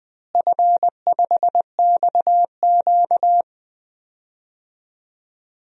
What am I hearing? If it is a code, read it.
Morse code, heard as a keyed tone, about 20 wpm, 700 Hz. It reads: F5XQ
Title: DXCC-100 Call Signs - 4 Characters - Repeated 3x - 20wpm